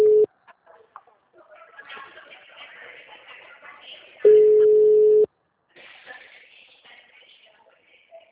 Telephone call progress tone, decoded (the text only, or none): ringback tone